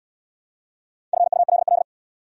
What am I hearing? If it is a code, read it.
Morse code, heard as a keyed tone, about 80 wpm, 700 Hz. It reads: FULL